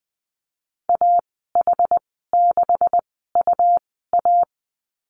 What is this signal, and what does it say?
Morse code, heard as a keyed tone, about 20 wpm, spaced as for 14 wpm, 700 Hz. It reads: AH6UA
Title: 231 Lesson 37g - Letter x - Call Signs 20wpm